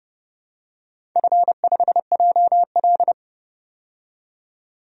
Morse code, heard as a keyed tone, about 30 wpm, 700 Hz. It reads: F5JL
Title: DXCC-100 Call Signs - 4 Characters - 30wpm